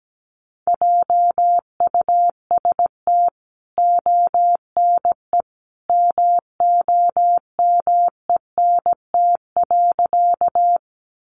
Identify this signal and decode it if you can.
Morse code, heard as a keyed tone, about 17 wpm, 700 Hz. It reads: JUST ONE MOMENT.
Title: Sentences from Top 1000 Words 17wpm